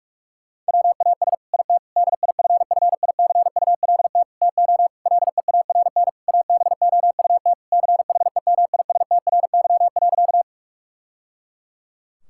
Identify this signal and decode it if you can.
Morse code, heard as a keyed tone, about 41 wpm, 700 Hz. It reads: WAS IT DIFFICULT TO LEARN ABOUT CHEMISTRY.
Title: Sentences from Top 1000 Words - Review Part 6 - 45wpm